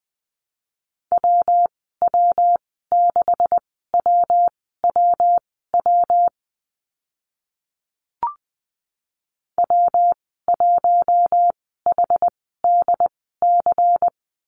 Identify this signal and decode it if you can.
Morse code, heard as a keyed tone, about 20 wpm, spaced as for 14 wpm, 700 Hz. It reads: WW6WWW W1HDC
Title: 140 Lesson 23g - Letter f - Call Signs 20wpm